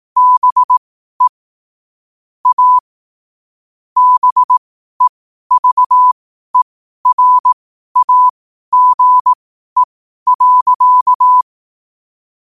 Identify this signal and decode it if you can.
Morse code, heard as a keyed tone, about 18 wpm, spaced as for 12 wpm, 1000 Hz. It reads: BE A BEVERAGE.
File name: qotdfiles / 12wpm